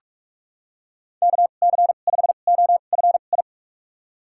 Morse code, heard as a keyed tone, about 46 wpm, spaced as for 32 wpm, 700 Hz. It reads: KC5OFI
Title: USA Call Signs - 6 Characters - 2x Character Spacing - 45wpm